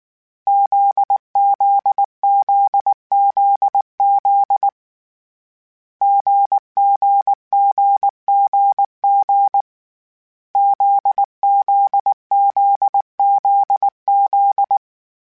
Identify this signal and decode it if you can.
Morse code, heard as a keyed tone, about 19 wpm, 800 Hz. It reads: ZZZZZ GGGGG ZZZZZ
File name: CW Code Groups-mp3s / Sets mp3s / GZ GZ7 sp @18